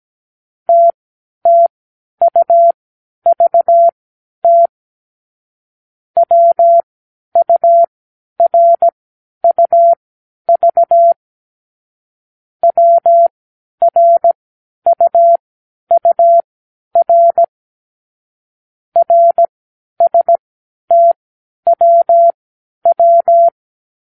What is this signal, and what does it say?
Morse code, heard as a keyed tone, about 17 wpm, spaced as for 10 wpm, 680 Hz. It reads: TTUVT WURUV WRUUR RSTWW
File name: k7qo / Lesson 36 (A-X)